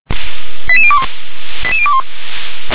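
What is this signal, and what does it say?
Five-tone selective call (ZVEI), heard as quick runs of five tones, tones of about 70 ms. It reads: 88021 88021
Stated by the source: ZVEI five-tone selective call received by SDR.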